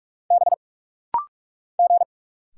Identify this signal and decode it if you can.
Morse code, heard as a keyed tone, about 46 wpm, 700 Hz. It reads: B G
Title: Single Letter-Number Alt - Rapid-Fire 45wpm